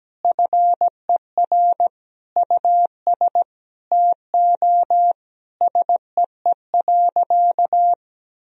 Morse code, heard as a keyed tone, about 17 wpm, 700 Hz. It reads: FER US TO SEE.
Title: Sentences from Top 500 Words and Abbreviations 17wpm